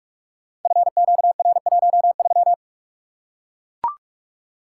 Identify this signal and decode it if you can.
Morse code, heard as a keyed tone, about 44 wpm, 700 Hz. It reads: FQR13
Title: Code Groups - 5 Letter:Numbers - 45wpm